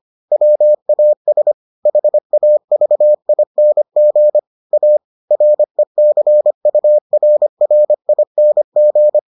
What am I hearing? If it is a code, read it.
Morse code, heard as a keyed tone, about 25 wpm, 600 Hz. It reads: WAS HAVING A RECURRING